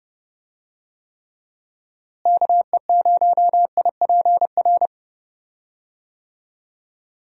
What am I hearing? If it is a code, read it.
Morse code, heard as a keyed tone, about 30 wpm, 700 Hz. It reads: KE0IPR
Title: USA Call Signs - 6 Characters - Repeated 3x - 30wpm